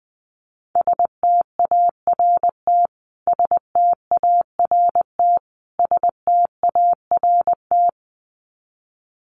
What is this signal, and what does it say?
Morse code, heard as a keyed tone, about 20 wpm, 700 Hz. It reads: START START START
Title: Top 300 Words Repeated 20wpm